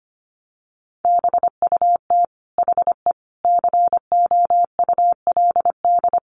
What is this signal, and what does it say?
Morse code, heard as a keyed tone, about 25 wpm, 700 Hz. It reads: BUT HE COULD
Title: Sentences from Top 1100 Words 25wpm